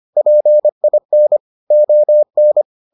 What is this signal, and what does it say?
Morse code, heard as a keyed tone, about 25 wpm, 600 Hz. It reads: PIN ON